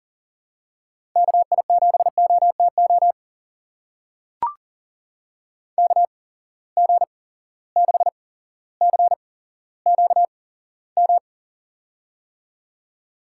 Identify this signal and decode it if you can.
Morse code, heard as a keyed tone, about 43 wpm, spaced as for 17 wpm, 700 Hz. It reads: KI7OTO KG6CQM
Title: USA Call Signs - 6 Characters - 8x Character Spacing - 40wpm